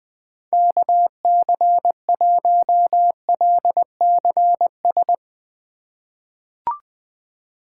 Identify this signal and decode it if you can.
Morse code, heard as a keyed tone, about 20 wpm, 700 Hz. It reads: KC1LCS
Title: USA Call Signs - 6 Characters - 2x Character Spacing - 20wpm